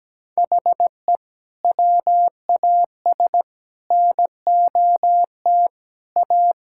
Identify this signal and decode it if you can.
Morse code, heard as a keyed tone, about 17 wpm, 700 Hz. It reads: HE WAS NOT A
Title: Sentences from Top 1000 Words - Review Part 5 - 17wpm